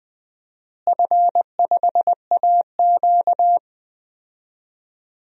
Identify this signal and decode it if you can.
Morse code, heard as a keyed tone, about 20 wpm, 700 Hz. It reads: F5AQ
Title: DXCC-100 Call Signs - 4 Characters - Repeated 2x - 20wpm